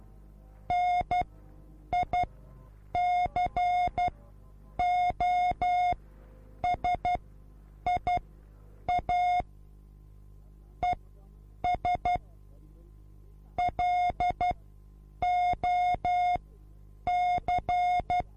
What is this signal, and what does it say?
Morse code, heard as a keyed tone, about 12 wpm, spaced as for 8 wpm, 720 Hz. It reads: NICOSIA ES LOC